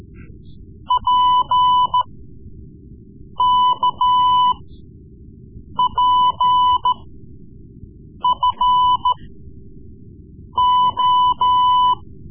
Morse code, heard as a keyed tone, about 14 wpm, spaced as for 5 wpm, 1000 Hz. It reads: PKPFO
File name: Morse Lessons / Morse Lessons / Tape-3B